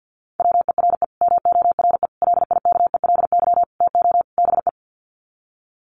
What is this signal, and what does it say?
Morse code, heard as a keyed tone, about 50 wpm, 700 Hz. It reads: WERE MORE LIKELY TO BE
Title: Sets of 5 Words Encore2 50wpm